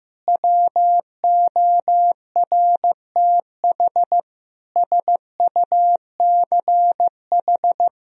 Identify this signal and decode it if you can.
Morse code, heard as a keyed tone, about 16 wpm, 700 Hz. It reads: WORTH SUCH
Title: Sentences from Top 1000 Words - Review Part 3 - 15wpm